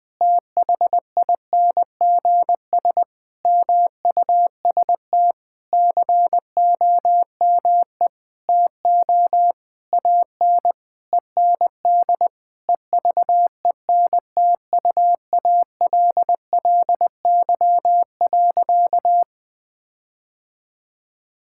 Morse code, heard as a keyed tone, about 20 wpm, 700 Hz. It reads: THINGS MUST COME TO AN END EVENTUALLY.
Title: Sentences with 8-Word Phrases - Spoken Beginning - 20wpm